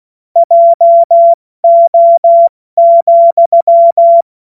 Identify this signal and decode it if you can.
Morse code, heard as a keyed tone, about 16 wpm, 680 Hz. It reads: JO,
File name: Witz-12-August-2025-70-F80